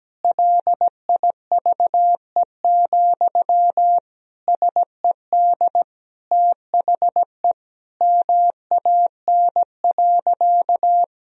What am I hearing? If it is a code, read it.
Morse code, heard as a keyed tone, about 17 wpm, 700 Hz. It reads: LIVE, SED THE MAN.
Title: Sentences from Top 200 Words and Abbreviations 17wpm